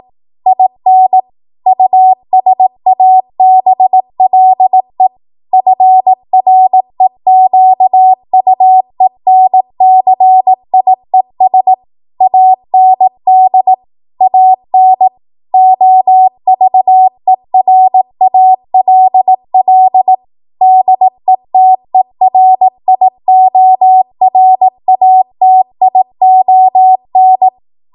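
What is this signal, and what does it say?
Morse code, heard as a keyed tone, about 18 wpm, 750 Hz. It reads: IN USABLE FREQUENCIES AND AN OVERALL DETERIORATION
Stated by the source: 18 WPM Morse code practice from a national amateur radio association's code archive.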